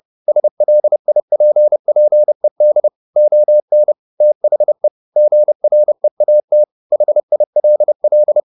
Morse code, heard as a keyed tone, about 30 wpm, 600 Hz. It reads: SLIPPED ON THE GREAT HILL
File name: the old man and the sea / the old man and the sea 1 30wpm